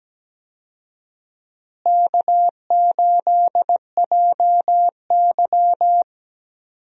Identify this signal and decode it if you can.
Morse code, heard as a keyed tone, about 17 wpm, 700 Hz. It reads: K8JY
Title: USA Call Signs - 4 Characters - Repeated 2x - 17wpm